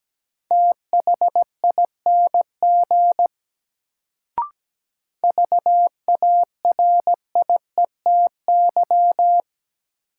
Morse code, heard as a keyed tone, about 17 wpm, 700 Hz. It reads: THING VARIETY
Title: Sets of 3 Words - New Word Drill - 17wpm